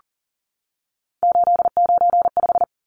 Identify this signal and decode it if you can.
Morse code, heard as a keyed tone, about 40 wpm, 700 Hz. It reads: 895
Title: Numbers - 3-Digit 40wpm